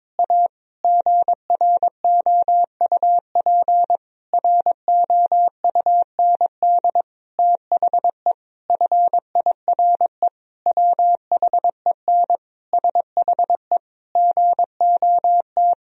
Morse code, heard as a keyed tone, about 22 wpm, 700 Hz. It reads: A GROUP ROUND THE FIRE WHEN SHE GOT
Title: Sentences from Top 1000 Words - Review Part 1 - 22wpm